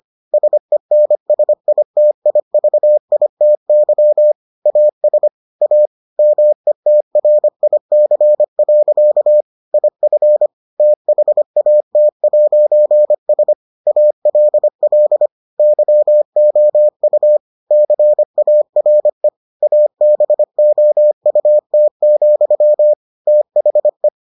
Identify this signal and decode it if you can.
Morse code, heard as a keyed tone, about 25 wpm, 600 Hz. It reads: SENSITIVITY AS A METRIC. IF THAT'S ALL YOU CARE ABOUT, THE